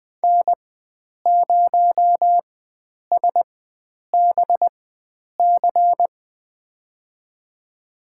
Morse code, heard as a keyed tone, about 20 wpm, spaced as for 9 wpm, 700 Hz. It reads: N0SBC
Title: USA Call Signs - 5 Characters - 4x Character Spacing - 20wpm